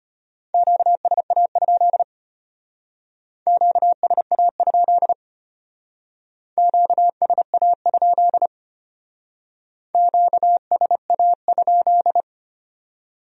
Morse code, heard as a keyed tone, about 32 wpm, 700 Hz. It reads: QSA? QSA? QSA? QSA?